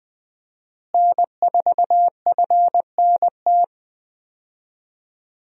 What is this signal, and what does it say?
Morse code, heard as a keyed tone, about 20 wpm, 700 Hz. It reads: N4FNT
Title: USA Call Signs - 5 Characters - Repeated 2x - 20wpm